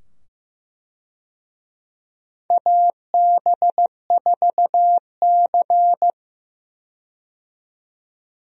Morse code, heard as a keyed tone, about 15 wpm, 700 Hz. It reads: AB4C
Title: USA Call Signs - 4 Characters - 15wpm